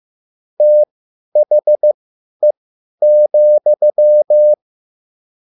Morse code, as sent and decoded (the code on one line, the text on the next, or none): - .... --..--
THE,